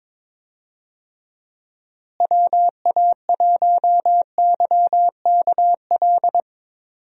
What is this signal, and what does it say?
Morse code, heard as a keyed tone, about 21 wpm, 700 Hz. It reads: WA1YKL
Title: USA Call Signs - 6 Characters - Repeated 2x - 22wpm